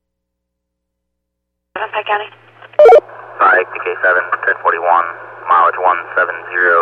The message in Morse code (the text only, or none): none